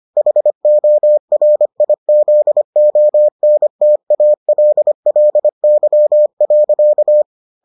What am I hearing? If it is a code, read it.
Morse code, heard as a keyed tone, about 25 wpm, 600 Hz. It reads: HORIZONTALLY.